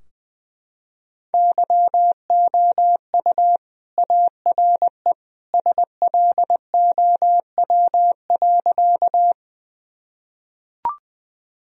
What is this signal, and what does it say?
Morse code, heard as a keyed tone, about 20 wpm, 700 Hz. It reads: YOU ARE SLOW.
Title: Sentences from Top 1000 Words - Review Part 2 - 20wpm